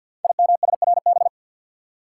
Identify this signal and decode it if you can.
Morse code, heard as a keyed tone, about 47 wpm, 700 Hz. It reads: IMSRB
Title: Code Groups - 5 Letters - 50wpm